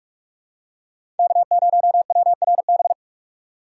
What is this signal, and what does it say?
Morse code, heard as a keyed tone, about 43 wpm, 700 Hz. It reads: K0WRB
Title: USA Call Signs - 4x Character Spacing - 45wpm